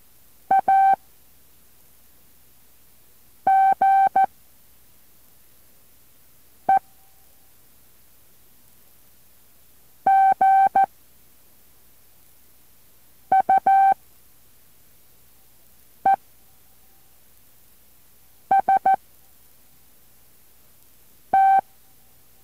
Morse code, heard as a keyed tone, about 14 wpm, 780 Hz. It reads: A G E G U E S T